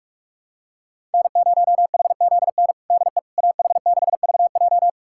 Morse code, heard as a keyed tone, about 46 wpm, 700 Hz. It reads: N0HZN DE AH6VJ
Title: USA Callsign DE Callsign - 5 Characters - 45wpm